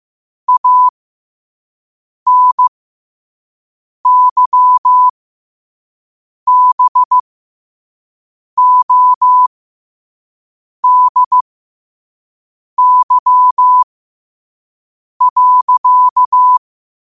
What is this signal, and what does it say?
Morse code, heard as a keyed tone, about 15 wpm, spaced as for 5 wpm, 1000 Hz. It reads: ANYBODY.
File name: qotdfiles / 5wpm